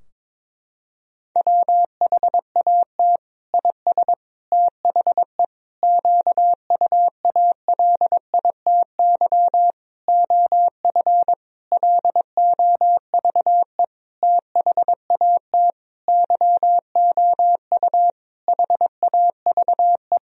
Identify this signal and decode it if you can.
Morse code, heard as a keyed tone, about 22 wpm, 700 Hz. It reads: WHAT IS THE QUALITY OF LOVE THAT YOU HAVE
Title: Sentences from Top 1000 Words - Review Part 4 - 22wpm